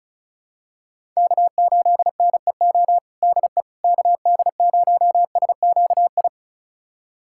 Morse code, heard as a keyed tone, about 36 wpm, 700 Hz. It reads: K8NEO DE KD0SQI
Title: USA Callsign DE Callsign 35wpm